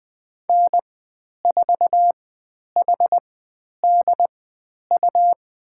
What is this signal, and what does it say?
Morse code, heard as a keyed tone, about 20 wpm, spaced as for 10 wpm, 700 Hz. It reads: N4HDU
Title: 102 Lesson 17g - Letter u - Call Signs 20wpm